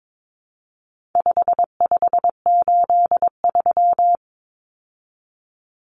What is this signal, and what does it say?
Morse code, heard as a keyed tone, about 22 wpm, 700 Hz. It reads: 5583
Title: Common Cut Numbers - 4-Digit 22wpm